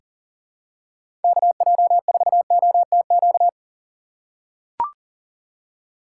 Morse code, heard as a keyed tone, about 39 wpm, 700 Hz. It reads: KJ4OTQ
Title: USA Call Signs - 6 Characters - 6x Character Spacing - 40wpm